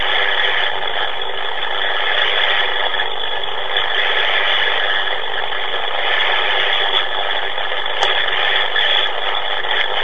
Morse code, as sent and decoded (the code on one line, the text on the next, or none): none